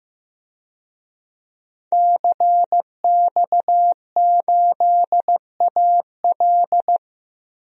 Morse code, heard as a keyed tone, about 15 wpm, 700 Hz. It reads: CX8AL
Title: DXCC-100 Call Signs - 5 Characters - 15wpm